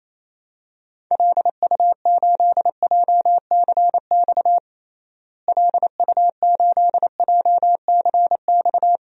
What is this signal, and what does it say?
Morse code, heard as a keyed tone, about 28 wpm, 700 Hz. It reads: LU8JCX LU8JCX
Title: DXCC-100 Call Signs - 6 Characters - Repeated 3x - 28wpm